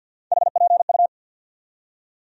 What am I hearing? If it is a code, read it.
Morse code, heard as a keyed tone, about 52 wpm, 700 Hz. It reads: HPU
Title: USA Call Sign Postfixes - 3 Letters 50wpm